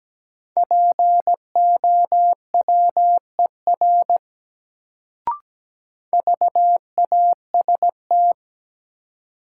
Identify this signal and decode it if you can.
Morse code, heard as a keyed tone, about 17 wpm, 700 Hz. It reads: POWER VAST
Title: Sets of 4 Words - New Word Drill - 17wpm